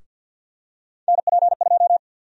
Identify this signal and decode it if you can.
Morse code, heard as a keyed tone, about 52 wpm, 700 Hz. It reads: NP2
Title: USA Call Sign Prefixes - 50wpm